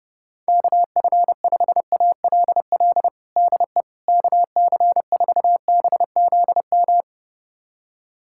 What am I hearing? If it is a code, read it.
Morse code, heard as a keyed tone, about 30 wpm, 700 Hz. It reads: KF5ALL DE KC4BZM
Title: USA Callsign DE Callsign - 6 Characters - 30wpm